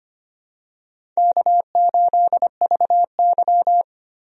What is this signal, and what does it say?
Morse code, heard as a keyed tone, about 25 wpm, 700 Hz. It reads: K8VY
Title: USA Call Signs - 4 Characters - 25wpm